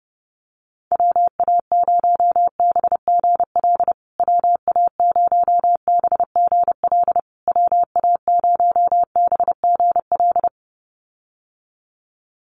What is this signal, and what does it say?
Morse code, heard as a keyed tone, about 30 wpm, 700 Hz. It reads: WA0BGL WA0BGL WA0BGL
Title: USA Call Signs Repeated 3x 30wpm